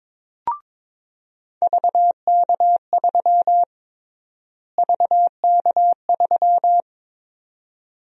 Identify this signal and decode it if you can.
Morse code, heard as a keyed tone, about 22 wpm, 700 Hz. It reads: VK3 VK3
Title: USA Call Sign Prefixes - Repeated 2x - 22wpm